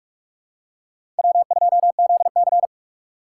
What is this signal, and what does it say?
Morse code, heard as a keyed tone, about 46 wpm, 700 Hz. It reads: W2ZC
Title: USA Call Signs - 4 Characters - 2x Character Spacing - 45wpm